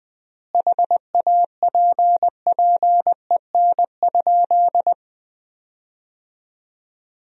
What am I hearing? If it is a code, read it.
Morse code, heard as a keyed tone, about 20 wpm, 700 Hz. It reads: HAPPEN?
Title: 4x Word-Spacing - Sentences from Top 400 Words - 20wpm